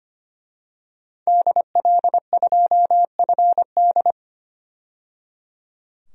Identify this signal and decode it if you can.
Morse code, heard as a keyed tone, about 25 wpm, 700 Hz. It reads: DL2FD
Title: DXCC-100 Call Signs - 5 Characters - 25wpm